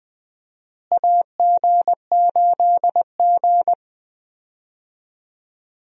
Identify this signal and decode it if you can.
Morse code, heard as a keyed tone, about 20 wpm, 700 Hz. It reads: AG8G
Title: USA Call Signs - 4 Characters - Repeated 3x - 20wpm